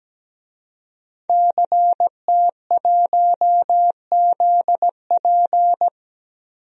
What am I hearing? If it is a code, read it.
Morse code, heard as a keyed tone, about 17 wpm, 700 Hz. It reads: CT1ZP
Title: DXCC-100 Call Signs - 5 Characters - 17wpm